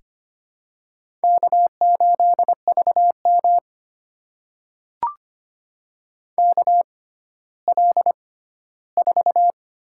Morse code, heard as a keyed tone, about 26 wpm, spaced as for 14 wpm, 700 Hz. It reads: K8VM KL4
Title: USA Call Signs - 4 Characters - 6x Character Spacing - 25wpm